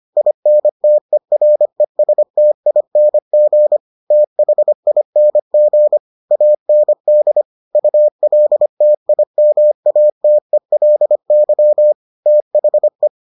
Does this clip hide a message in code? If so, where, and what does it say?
Yes: Morse code from 0.2 to 13.1 s, heard as a keyed tone, about 25 wpm, 600 Hz. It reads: INTERESTING THING AND ULTIMATELY THE